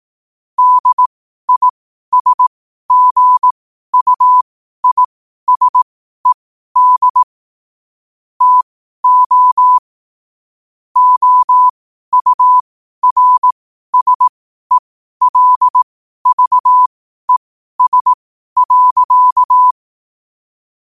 Morse code, heard as a keyed tone, about 18 wpm, spaced as for 12 wpm, 1000 Hz. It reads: DISGUISED TO OURSELVES.
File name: qotdfiles / 12wpm